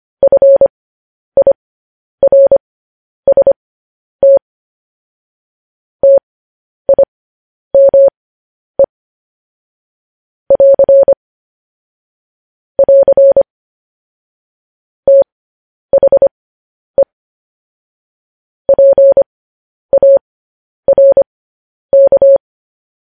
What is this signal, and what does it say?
Morse code, heard as a keyed tone, about 25 wpm, spaced as for 10 wpm, 550 Hz. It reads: FIRST TIME <AR> <AR> THE PARK
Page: news-headlines-in-morse-code-at-10-wpm